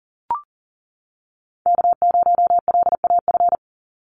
Morse code, heard as a keyed tone, about 41 wpm, 700 Hz. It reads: K0LAF
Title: NA SOTA Chaser Callsigns 40wpm